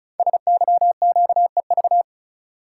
Morse code, heard as a keyed tone, about 36 wpm, 690 Hz. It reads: SYQEV
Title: Code Groups - 5 Letters - 35wpm